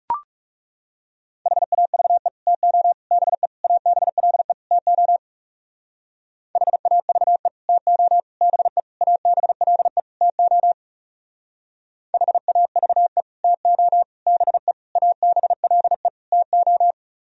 Morse code, heard as a keyed tone, about 36 wpm, 700 Hz. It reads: HAVE TO BE ABLE TO HAVE TO BE ABLE TO HAVE TO BE ABLE TO
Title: Speed-Racing - Sets of 5 Words 30wpm